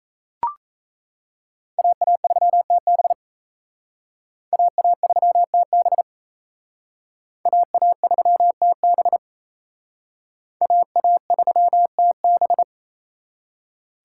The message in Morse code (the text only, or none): AA3TB AA3TB AA3TB AA3TB